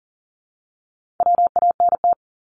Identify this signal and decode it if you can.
Morse code, heard as a keyed tone, about 40 wpm, 700 Hz. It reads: WANT
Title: ED Suffix - Top 100 Words with Base Words 40wpm